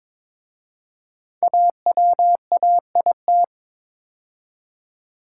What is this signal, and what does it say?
Morse code, heard as a keyed tone, about 22 wpm, 700 Hz. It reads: AWAIT
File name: 5 Letter Words - First 3 Letters Given - 22wpm